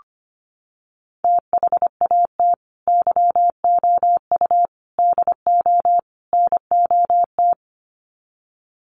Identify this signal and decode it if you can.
Morse code, heard as a keyed tone, about 25 wpm, 700 Hz. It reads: THAT YOU DO NOT
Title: Sets of 4 Words 25wpm